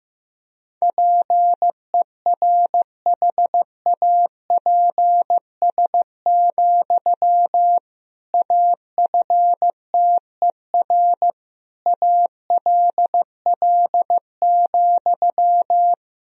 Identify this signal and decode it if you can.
Morse code, heard as a keyed tone, about 16 wpm, 700 Hz. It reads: PERHAPS, AFTER ALL,
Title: Sentences from Top 500 Words and Abbreviations 15wpm